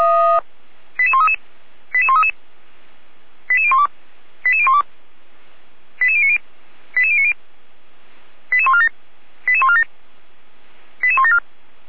Five-tone selective call (ZVEI), heard as quick runs of five tones, tones of about 80 ms. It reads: 88133 88133 88012 88012 88090 88090 88147 88147 88174